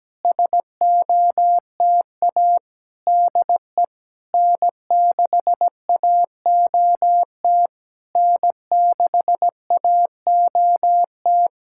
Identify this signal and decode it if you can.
Morse code, heard as a keyed tone, about 17 wpm, 700 Hz. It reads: SOTA DE N6AOT N6AOT
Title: SOTA QSO Practice 17wpm